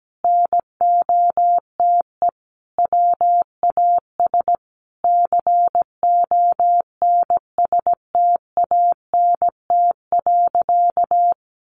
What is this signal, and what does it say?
Morse code, heard as a keyed tone, about 17 wpm, 700 Hz. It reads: NOTE WAS CONSTANT.
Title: Sentences from Top 1500 Words 17wpm